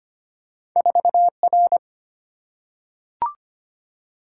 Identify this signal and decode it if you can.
Morse code, heard as a keyed tone, about 25 wpm, 700 Hz. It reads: 4R
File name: Two Letter-Number Combinations - 25wpm